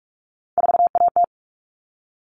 Morse code, heard as a keyed tone, about 44 wpm, 700 Hz. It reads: 4AT
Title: All Cut Numbers - 3-Digit 45wpm